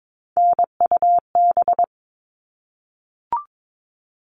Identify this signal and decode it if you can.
Morse code, heard as a keyed tone, about 22 wpm, 700 Hz. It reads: NUB
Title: All Cut Numbers - 3-Digit 22wpm